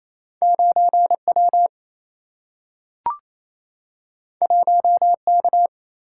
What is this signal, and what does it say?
Morse code, heard as a keyed tone, about 28 wpm, 700 Hz. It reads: 9W 1K
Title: Single-Letter Single-Number 28wpm